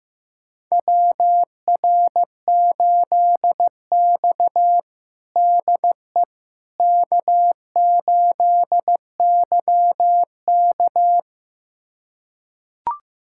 Morse code, heard as a keyed tone, about 15 wpm, 700 Hz. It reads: WR8X DE K8YK
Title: USA Callsign DE Callsign - 4 Characters - 15wpm